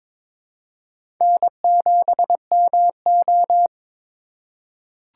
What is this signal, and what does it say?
Morse code, heard as a keyed tone, about 22 wpm, 700 Hz. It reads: N7MO
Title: USA Call Signs - 2x Character Spacing - 22wpm